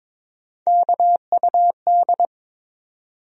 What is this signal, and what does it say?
Morse code, heard as a keyed tone, about 22 wpm, 700 Hz. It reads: KUD
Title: USA Call Sign Postfixes - 3 Letters 22wpm